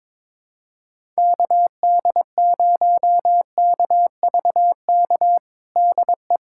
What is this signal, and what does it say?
Morse code, heard as a keyed tone, about 22 wpm, 700 Hz. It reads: KD0KVK DE
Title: USA Callsign DE Callsign - 6 Characters - 22wpm